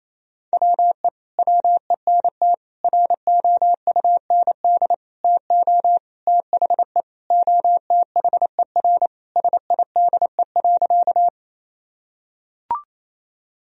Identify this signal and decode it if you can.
Morse code, heard as a keyed tone, about 28 wpm, 700 Hz. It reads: WE WENT ROUND TO THE OTHER SIDE.
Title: Sentences from Top 1000 Words - Review Part 1 - 28wpm